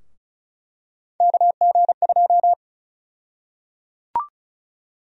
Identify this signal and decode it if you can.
Morse code, heard as a keyed tone, about 36 wpm, 700 Hz. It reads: KG2
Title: USA Call Sign Prefixes - Repeated 2x - 35wpm